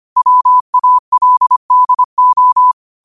Morse code, heard as a keyed tone, about 25 wpm, 1000 Hz. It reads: WALDO
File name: 25wpm